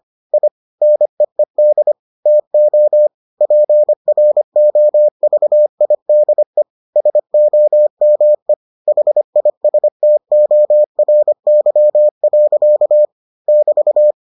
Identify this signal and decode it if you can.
Morse code, heard as a keyed tone, about 25 wpm, 600 Hz. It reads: I NEED TO PROVIDE SOME HISTORY. <BT>